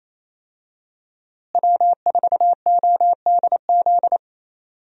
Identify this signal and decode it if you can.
Morse code, heard as a keyed tone, about 28 wpm, 700 Hz. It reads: W4ODZ
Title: USA Call Signs - 5 Characters - Repeated 3x - 28wpm